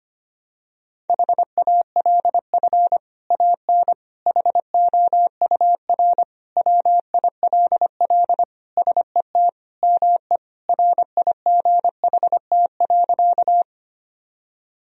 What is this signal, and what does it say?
Morse code, heard as a keyed tone, about 25 wpm, 700 Hz. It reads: HALF AN HOUR WILL SET ME RIGHT.
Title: Sentences from Top 1000 Words - Review Part 1 - 25wpm